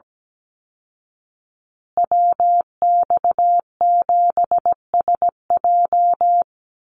Morse code, heard as a keyed tone, about 17 wpm, 700 Hz. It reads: WX7SJ
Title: Call Signs Repeated 2x - 17wpm